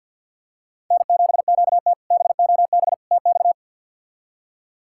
Morse code, heard as a keyed tone, about 50 wpm, 700 Hz. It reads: N7QT BOB TX
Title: K1USN SST QSO Practice - 50wpm